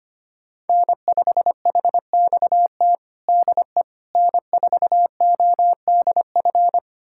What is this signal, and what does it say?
Morse code, heard as a keyed tone, about 25 wpm, 700 Hz. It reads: N5HXT DE N4ODF
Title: USA Callsign DE Callsign - 5 Characters - 25wpm